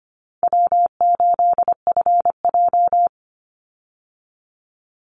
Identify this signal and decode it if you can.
Morse code, heard as a keyed tone, about 24 wpm, 700 Hz. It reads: W8FJ
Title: Easy Call Signs - 25wpm